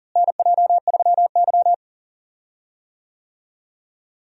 Morse code, heard as a keyed tone, about 40 wpm, 700 Hz. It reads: NJ3Y
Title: USA Call Signs - 4 Characters - Repeated 3x - 40wpm